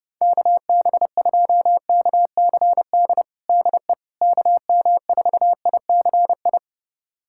Morse code, heard as a keyed tone, about 32 wpm, 700 Hz. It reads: KB2KCD DE KM4ICI
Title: USA Callsign DE Callsign 30wpm